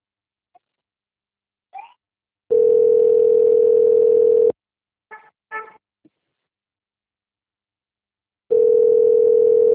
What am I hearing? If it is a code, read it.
Ringback tone (North America), 440 + 480 Hz.